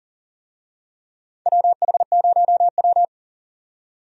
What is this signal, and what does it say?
Morse code, heard as a keyed tone, about 40 wpm, 700 Hz. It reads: WH0W
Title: USA Call Signs - 4 Characters - 40wpm